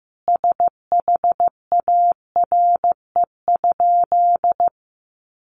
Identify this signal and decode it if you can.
Morse code, heard as a keyed tone, about 15 wpm, 700 Hz. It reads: SHARE?
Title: Sentences from Top 900 Words 15wpm